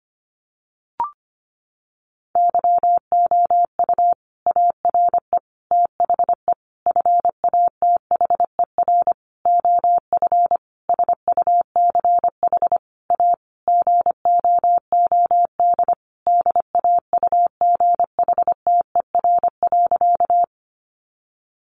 Morse code, heard as a keyed tone, about 25 wpm, 700 Hz. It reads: YOU ARE THE FATHER OF SUCH A GOOD DAUGHTER.
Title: Sentences from Top 1500 Words 25wpm